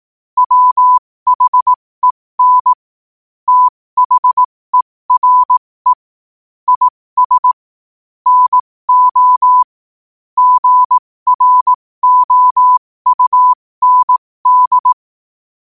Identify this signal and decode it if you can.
Morse code, heard as a keyed tone, about 18 wpm, spaced as for 15 wpm, 1000 Hz. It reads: WHEN THERE IS NO GROUND
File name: qotdfiles / 15wpm